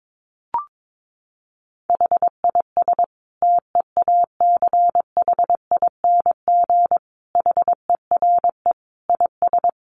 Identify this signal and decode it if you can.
Morse code, heard as a keyed tone, about 22 wpm, 700 Hz. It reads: HIS TEACHING HERE IS
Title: Sentences from Top 1100 Words 22wpm